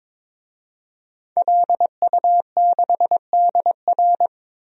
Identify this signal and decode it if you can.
Morse code, heard as a keyed tone, about 22 wpm, 700 Hz. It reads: LU6DR
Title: DXCC-100 Call Signs - 5 Characters - 22wpm